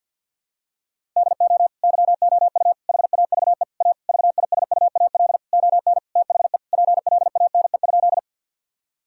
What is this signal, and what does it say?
Morse code, heard as a keyed tone, about 50 wpm, 700 Hz. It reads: DO YOU HAVE A VISUAL ON THE PLANE?
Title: Sentences from Top 1000 Words - Review Part 6 - 50wpm